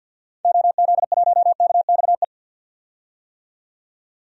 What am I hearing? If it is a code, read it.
Morse code, heard as a keyed tone, about 46 wpm, 700 Hz. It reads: OZ1KXE
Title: DXCC-100 Call Signs - 6 Characters - 50wpm